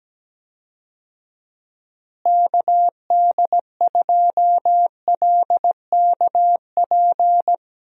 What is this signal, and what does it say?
Morse code, heard as a keyed tone, about 17 wpm, 700 Hz. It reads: KD2LKP
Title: USA Call Signs - 6 Characters - Repeated 3x - 17wpm